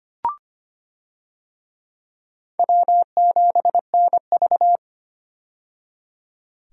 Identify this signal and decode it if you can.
Morse code, heard as a keyed tone, about 25 wpm, 700 Hz. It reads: W7NV
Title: USA Call Signs - 4 Characters - 25wpm